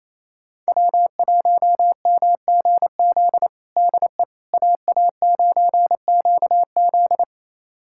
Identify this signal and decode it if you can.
Morse code, heard as a keyed tone, about 27 wpm, 700 Hz. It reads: W1MGZ DE AA9QZ
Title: USA Callsign DE Callsign - 5 Characters - 28wpm